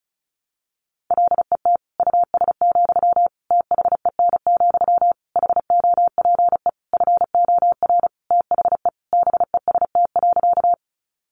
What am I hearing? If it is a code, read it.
Morse code, heard as a keyed tone, about 34 wpm, 700 Hz. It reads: LET US, THEN, HOPE FOR THE BEST.
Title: Sentences from Top 1000 Words 35wpm